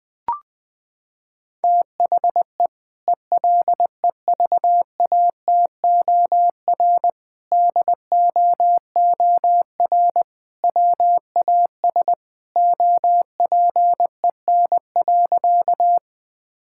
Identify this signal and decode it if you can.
Morse code, heard as a keyed tone, about 21 wpm, 700 Hz. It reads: THE ELEVATOR DOOR WAS OPEN.
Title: Sentences from Top 1600 Words 20wpm